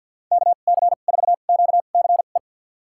Morse code, heard as a keyed tone, about 47 wpm, spaced as for 35 wpm, 690 Hz. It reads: KC4QCE
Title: USA Call Signs - 2x Character Spacing - 50wpm